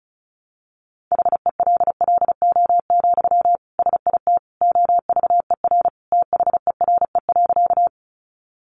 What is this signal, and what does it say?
Morse code, heard as a keyed tone, about 34 wpm, 700 Hz. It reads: HELLO, SIT OVER THERE.
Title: Sentences from Top 1100 Words 35wpm